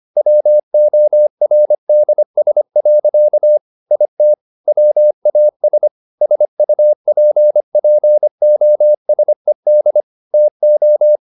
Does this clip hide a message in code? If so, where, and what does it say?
Yes: Morse code from 0.2 to 11.2 s, heard as a keyed tone, about 25 wpm, 600 Hz. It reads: WORDS. IT WAS SUPPOSED TO